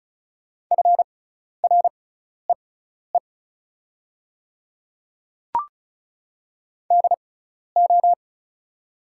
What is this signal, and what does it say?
Morse code, heard as a keyed tone, about 34 wpm, spaced as for 12 wpm, 690 Hz. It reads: FREE DO
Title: 6x Character Spacing - Top 500 Words - 35wpm